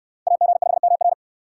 Morse code, heard as a keyed tone, about 69 wpm, 700 Hz. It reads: AG5MR